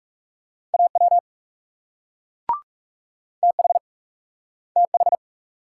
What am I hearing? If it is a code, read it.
Morse code, heard as a keyed tone, about 42 wpm, 700 Hz. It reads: AW TH TH